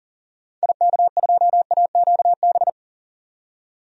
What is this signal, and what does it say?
Morse code, heard as a keyed tone, about 40 wpm, 700 Hz. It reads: IK2AQB
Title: DXCC-100 Call Signs - 6 Characters - 40wpm